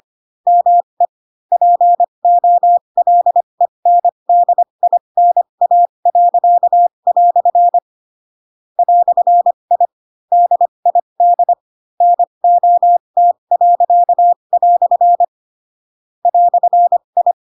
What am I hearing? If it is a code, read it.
Morse code, heard as a keyed tone, about 25 wpm, spaced as for 22 wpm, 700 Hz. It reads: ME POLENDINA." "I DID NOT." "I
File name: Pinocchio / Pinocchio ch2 18 25 ik1hge